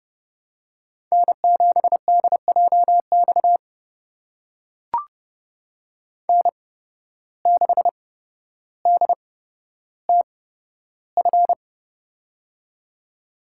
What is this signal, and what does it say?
Morse code, heard as a keyed tone, about 31 wpm, spaced as for 13 wpm, 700 Hz. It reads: N7DJX N6DTF
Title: USA Call Signs - 8x Character Spacing - 30wpm